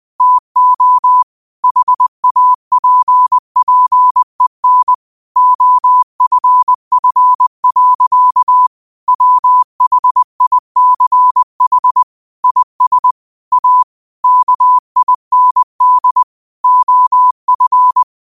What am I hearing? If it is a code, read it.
Morse code, heard as a keyed tone, about 20 wpm, 1000 Hz. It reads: TO HAPPEN OFF. WHICH IS A KIND OF
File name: 20wpm